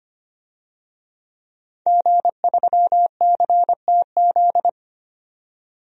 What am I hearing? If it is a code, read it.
Morse code, heard as a keyed tone, about 25 wpm, 700 Hz. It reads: G3CTZ
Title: DXCC-100 Call Signs - 5 Characters - 25wpm